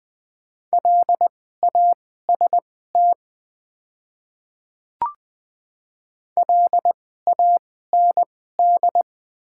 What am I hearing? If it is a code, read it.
Morse code, heard as a keyed tone, about 20 wpm, spaced as for 14 wpm, 690 Hz. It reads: LAST LAND
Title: 2x Character Spacing - Top 300 Words - 20wpm